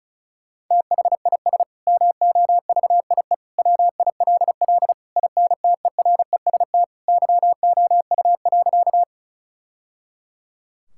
Morse code, heard as a keyed tone, about 36 wpm, 700 Hz. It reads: THIS MOVIE WILL INTEREST YOU.
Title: Sentences from Top 1000 Words - Review Part 5 - 35wpm